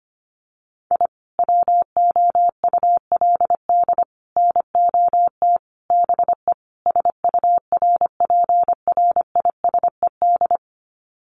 Morse code, heard as a keyed tone, about 25 wpm, 700 Hz. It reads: I WOULD NOT BE SURPRISED
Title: Sets of 5 Words Encore2 25wpm